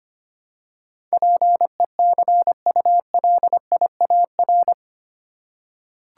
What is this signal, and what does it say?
Morse code, heard as a keyed tone, about 25 wpm, 700 Hz. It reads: PECULIAR
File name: Eight Letter Words - 25wpm